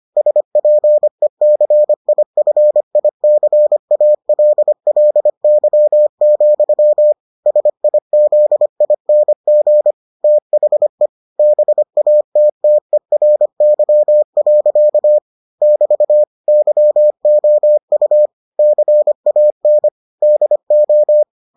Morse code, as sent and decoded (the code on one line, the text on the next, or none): ... .--. -.-. .. ..-. .. -.-. .- .-.. .-.. -.-- --..-- / ... .. --.. .. -. --. / - .... / -... .- - - . .-. -.-- .-.-.- / -...- / -.-- --- ..- / -.-. .- -. / -.. ---
SPECIFICALLY, SIZING THE BATTERY. <BT> YOU CAN DO